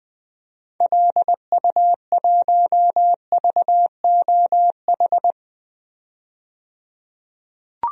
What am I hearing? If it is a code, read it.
Morse code, heard as a keyed tone, about 20 wpm, 700 Hz. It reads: LU1VOH
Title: DXCC-100 Call Signs - 6 Characters - 20wpm